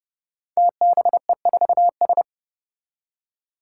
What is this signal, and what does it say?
Morse code, heard as a keyed tone, about 30 wpm, 700 Hz. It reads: TBE4S
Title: Five Challenging Letter-Number Combinations 30wpm